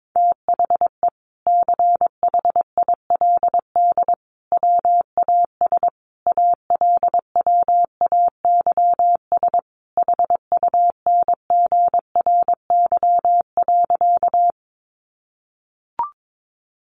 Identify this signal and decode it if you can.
Morse code, heard as a keyed tone, about 22 wpm, 700 Hz. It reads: THE CHILD WAS ALWAYS HUNGRY.
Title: Sentences from Top 1400 Words 22wpm